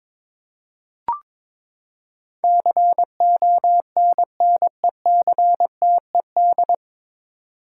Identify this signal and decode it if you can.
Morse code, heard as a keyed tone, about 21 wpm, 700 Hz. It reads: CONNECTED
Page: Morse Code Ninja practice 22wpm